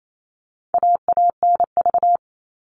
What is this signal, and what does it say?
Morse code, heard as a keyed tone, about 28 wpm, 700 Hz. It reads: AANV
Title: All Cut Numbers - 4-Digit 28wpm